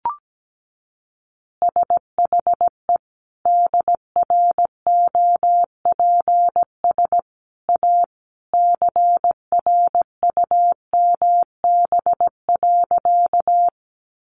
Morse code, heard as a keyed tone, about 17 wpm, 700 Hz. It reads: SHE DROPS A CRUMB.
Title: Easy Phrases 17wpm